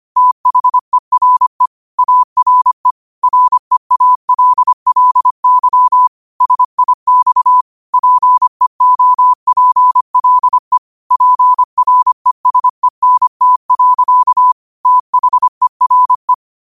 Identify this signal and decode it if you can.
Morse code, heard as a keyed tone, about 25 wpm, 1000 Hz. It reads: THERE ARE REALLY SIX PEOPLE PRESENT. THERE
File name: qotdfiles / 25wpm